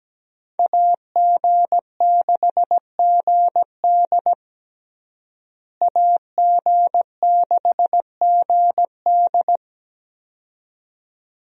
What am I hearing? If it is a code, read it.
Morse code, heard as a keyed tone, about 17 wpm, 700 Hz. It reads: AG6GD AG6GD
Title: USA Call Signs - 5 Characters - Repeated 2x - 17wpm